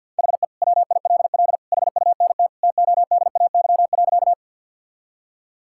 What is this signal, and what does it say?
Morse code, heard as a keyed tone, about 49 wpm, 700 Hz. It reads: HE WILL HUNT TODAY.
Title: Sentences from Top 1600 Words 50wpm